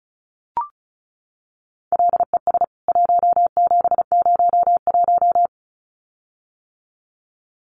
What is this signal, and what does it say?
Morse code, heard as a keyed tone, about 34 wpm, 700 Hz. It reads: LES 1701
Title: CWT Exchange 35wpm